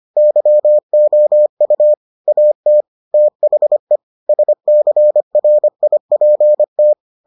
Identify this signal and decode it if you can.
Morse code, heard as a keyed tone, about 25 wpm, 600 Hz. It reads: YOU AT THE SCRIPT